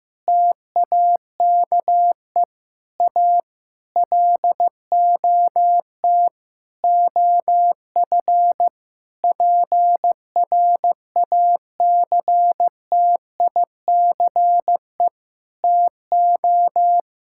Morse code, heard as a keyed tone, about 15 wpm, 700 Hz. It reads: TAKE A LOT OF PRACTICE TO
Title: Sentences from Top 1000 Words - Review Part 6 - 15wpm